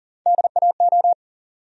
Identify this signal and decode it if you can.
Morse code, heard as a keyed tone, about 40 wpm, 700 Hz. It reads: DAO
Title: USA Call Sign Postfixes - 3 Letters 40wpm